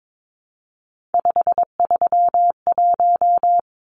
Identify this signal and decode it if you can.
Morse code, heard as a keyed tone, about 22 wpm, 700 Hz. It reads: 531
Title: Numbers - 3-Digit 22wpm